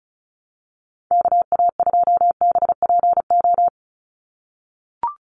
Morse code, heard as a keyed tone, about 34 wpm, 700 Hz. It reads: KA2BPO
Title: USA Call Signs 35wpm